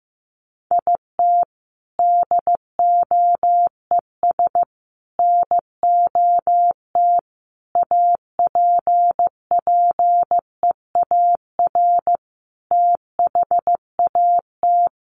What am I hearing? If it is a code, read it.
Morse code, heard as a keyed tone, about 15 wpm, 700 Hz. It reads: IT DOES NOT APPEAR THAT
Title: Sentences from Top 1400 Words 15wpm